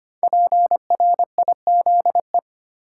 Morse code, heard as a keyed tone, about 25 wpm, 700 Hz. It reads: PRIZE
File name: 5 Letter Words - First 3 Letters Given - 25wpm